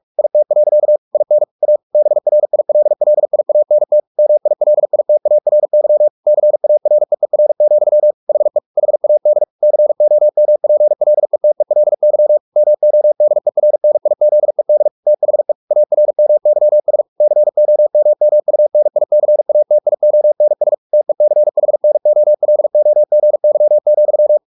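Morse code, heard as a keyed tone, about 44 wpm, 600 Hz. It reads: IT. IN A BRILLIANT MILITARY CAREER, HE HAD COMPLETELY MODERNIZED THE ARMYS COMMUNICATIONS TECHNOLOGY,
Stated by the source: Morse practice at 45 wpm, the novel Never by Ken Follett in English, from CW med Gnister.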